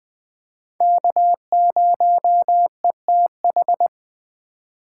Morse code, heard as a keyed tone, about 22 wpm, 700 Hz. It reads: K0ETH